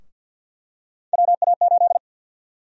Morse code, heard as a keyed tone, about 47 wpm, 700 Hz. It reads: WA8
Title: USA Call Sign Prefixes - 50wpm